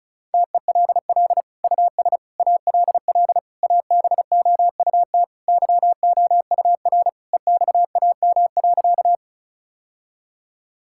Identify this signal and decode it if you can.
Morse code, heard as a keyed tone, about 34 wpm, 700 Hz. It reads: TELL US ALL ABOUT YOUR EXAM.
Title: Sentences from Top 1000 Words - Review Part 5 - 35wpm